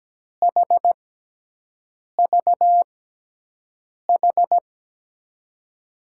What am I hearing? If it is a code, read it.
Morse code, heard as a keyed tone, about 17 wpm, spaced as for 6 wpm, 700 Hz. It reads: HVH
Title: USA Call Signs - 6 Characters - 6x Character Spacing - 17wpm